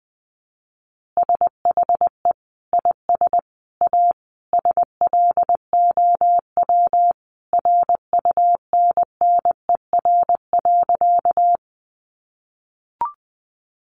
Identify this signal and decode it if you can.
Morse code, heard as a keyed tone, about 20 wpm, 700 Hz. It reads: SHE IS A SLOW RUNNER.
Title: Additional Easy Phrases 20wpm